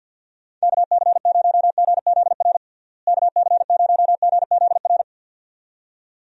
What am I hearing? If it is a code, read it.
Morse code, heard as a keyed tone, about 50 wpm, 700 Hz. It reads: KC0GZR KC0GZR
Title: USA Call Signs - 6 Characters - Repeated 2x - 50wpm